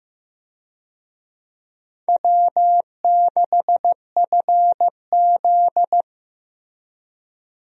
Morse code, heard as a keyed tone, about 15 wpm, 700 Hz. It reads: W6FZ
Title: USA Call Signs - 4 Characters - Repeated 2x - 15wpm